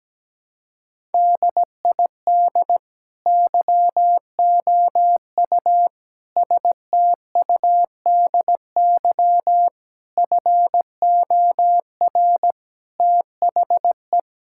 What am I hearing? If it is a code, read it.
Morse code, heard as a keyed tone, about 17 wpm, 700 Hz. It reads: DID YOU STUDY FOR THE
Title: Sentences from Top 1000 Words - Review Part 5 - 17wpm